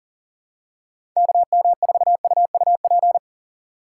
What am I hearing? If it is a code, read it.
Morse code, heard as a keyed tone, about 40 wpm, 700 Hz. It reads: KM4UUP
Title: USA Call Signs - 6 Characters - 6x Character Spacing - 40wpm